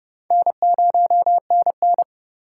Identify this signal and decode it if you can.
Morse code, heard as a keyed tone, about 29 wpm, 700 Hz. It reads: N0NN